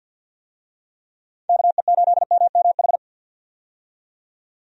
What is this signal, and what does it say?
Morse code, heard as a keyed tone, about 49 wpm, 700 Hz. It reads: KE8MMH
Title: USA Call Signs - 6 Characters - 50wpm